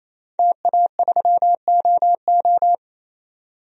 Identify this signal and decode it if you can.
Morse code, heard as a keyed tone, about 28 wpm, 700 Hz. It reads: TA3OO
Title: DXCC-100 Call Signs - 5 Characters - 28wpm